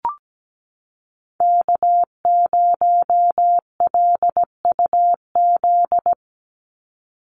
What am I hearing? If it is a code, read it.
Morse code, heard as a keyed tone, about 17 wpm, 700 Hz. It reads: K0LUZ
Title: Call Signs - 17wpm